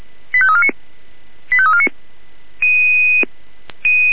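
Five-tone selective call (ZVEI), heard as quick runs of five tones, tones of about 70 ms. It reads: 85349 85349
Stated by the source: ZVEI five-tone selective call received on SDR.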